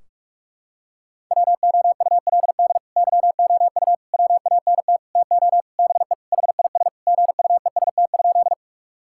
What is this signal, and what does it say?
Morse code, heard as a keyed tone, about 41 wpm, 700 Hz. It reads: WOULD YOU WANT TO BE HIS GUEST?
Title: Sentences from Top 1600 Words 45wpm